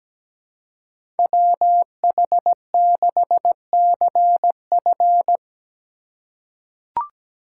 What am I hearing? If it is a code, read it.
Morse code, heard as a keyed tone, about 17 wpm, 700 Hz. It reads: WH6CF
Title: USA Call Signs - 2x Character Spacing - 17wpm